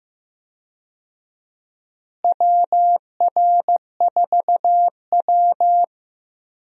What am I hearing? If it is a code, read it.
Morse code, heard as a keyed tone, about 15 wpm, 700 Hz. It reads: WR4W
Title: USA Call Signs - 4 Characters - Repeated 3x - 15wpm